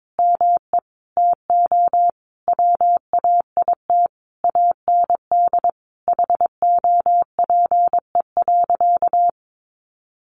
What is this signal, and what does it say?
Morse code, heard as a keyed tone, about 22 wpm, 700 Hz. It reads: ME TO WAIT AND HOPE.
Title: Sentences from Top 1000 Words 22wpm